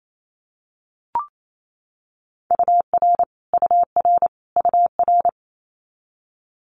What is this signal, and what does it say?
Morse code, heard as a keyed tone, about 28 wpm, 700 Hz. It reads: UR UR UR
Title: Multi-letter Phonemes Repeated 3x 28wpm